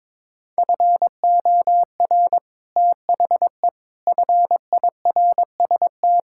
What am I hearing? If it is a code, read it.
Morse code, heard as a keyed tone, about 22 wpm, 700 Hz. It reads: FOR THE FIRST
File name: practice / 22wpm / Sets of 6 Words - 22wpm